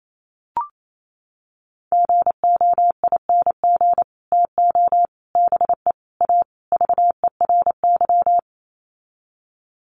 Morse code, heard as a keyed tone, about 28 wpm, 700 Hz. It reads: GOING TO BE A VERY
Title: Sets of 5 Words Encore 28wpm